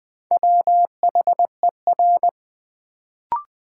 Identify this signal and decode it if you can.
Morse code, heard as a keyed tone, about 20 wpm, 700 Hz. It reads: WHER
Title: Sentences from Top 1500 Words - New Word Drill Alt - 20wpm